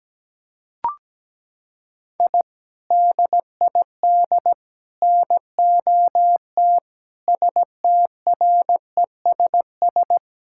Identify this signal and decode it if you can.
Morse code, heard as a keyed tone, about 17 wpm, 700 Hz. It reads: I DID NOT STRESS
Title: Sentences from Top 1000 Words - Review Part 6 - 17wpm